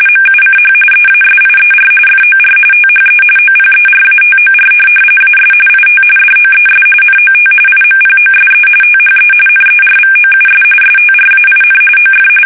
RTTY, about 45.45 baud. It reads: MUCH EASIER FOR THE OPERATOR AS IT DOES NOT IMPOSE THE NERVOUS STRAIN OF
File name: fsk850